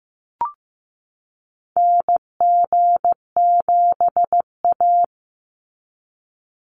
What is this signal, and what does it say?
Morse code, heard as a keyed tone, about 15 wpm, 700 Hz. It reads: NG7A
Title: NA SOTA Activator Callsigns 15wpm